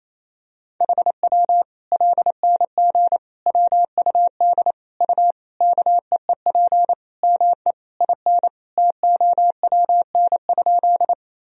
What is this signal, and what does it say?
Morse code, heard as a keyed tone, about 28 wpm, 700 Hz. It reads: HW LNG WUD U KEEP ME IN TOWN?
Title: Sentences from Top 500 Words and Abbreviations 28wpm